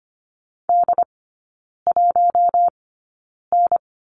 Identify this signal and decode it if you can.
Morse code, heard as a keyed tone, about 25 wpm, spaced as for 9 wpm, 700 Hz. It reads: D1N
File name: course / 25:10wpm / 093 Lesson 16e - Letter c - Three Characters 25wpm